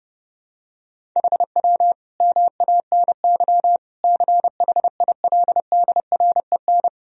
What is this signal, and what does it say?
Morse code, heard as a keyed tone, about 32 wpm, 700 Hz. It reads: HW MANY CHILDREN
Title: Sentences from Top 400 Words and Abbreviations 30wpm